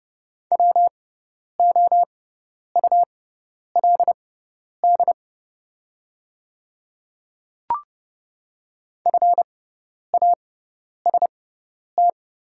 Morse code, heard as a keyed tone, about 30 wpm, spaced as for 10 wpm, 700 Hz. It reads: WOULD FAST
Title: 6x Character Spacing - Top 500 Words - 30wpm